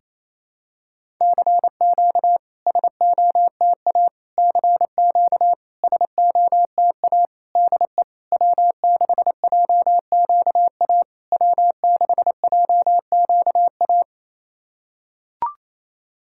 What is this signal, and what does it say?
Morse code, heard as a keyed tone, about 28 wpm, 700 Hz. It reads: CQ SOTA CQ SOTA DE W6JQA W6JQA
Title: SOTA QSO Practice 28wpm